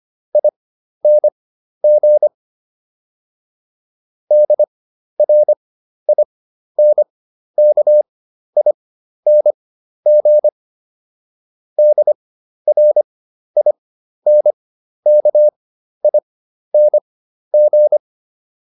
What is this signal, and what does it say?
Morse code, heard as a keyed tone, about 25 wpm, spaced as for 12 wpm, 600 Hz. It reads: ING DRINKING DRINKING